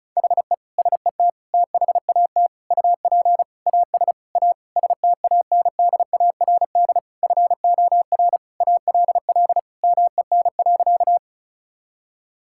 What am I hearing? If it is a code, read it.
Morse code, heard as a keyed tone, about 36 wpm, 700 Hz. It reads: HE SET THAT UP AS A STANDARD FOR ALL MEN.
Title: Sentences from Top 1000 Words - Review Part 6 - 35wpm